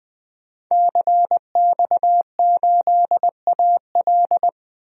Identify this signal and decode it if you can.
Morse code, heard as a keyed tone, about 20 wpm, 700 Hz. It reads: CX8AL
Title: DXCC-100 Call Signs - 5 Characters - 20wpm